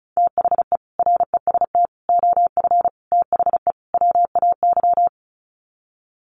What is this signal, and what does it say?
Morse code, heard as a keyed tone, about 34 wpm, 700 Hz. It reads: THE REST OF THE WAY
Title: Sets of 5 Words Encore 35wpm